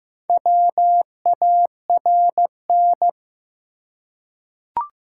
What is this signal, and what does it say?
Morse code, heard as a keyed tone, about 15 wpm, 700 Hz. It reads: WARN
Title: Four Letter Words - 15wpm